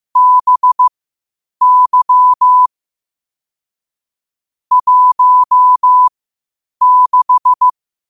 Morse code, heard as a keyed tone, about 15 wpm, spaced as for 8 wpm, 1000 Hz. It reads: BY 16
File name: qotdfiles / 8wpm